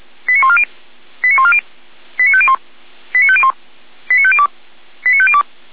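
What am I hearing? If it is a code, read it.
Five-tone selective call (ZVEI), heard as quick runs of five tones, tones of about 70 ms. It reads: 89140 89140 89501 89501 89502 89502
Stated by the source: ZVEI five-tone selective call 89052 received by SDR, served from the Feuerwehr Orlamünde audio server.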